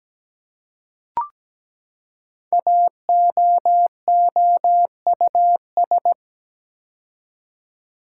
Morse code, heard as a keyed tone, about 17 wpm, 700 Hz. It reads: AOOUS
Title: Code Groups - 5 Letters - 17wpm